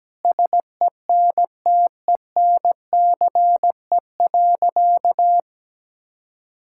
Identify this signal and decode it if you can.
Morse code, heard as a keyed tone, about 17 wpm, 700 Hz. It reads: SENTENCE.
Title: Sentences from Top 1000 Words - Review Part 6 - 17wpm